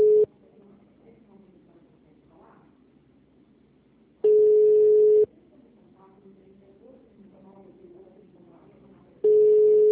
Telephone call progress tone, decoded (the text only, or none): ringback tone